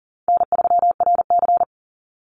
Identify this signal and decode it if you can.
Morse code, heard as a keyed tone, about 40 wpm, 700 Hz. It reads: N3RC
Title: Easy Call Signs - 40wpm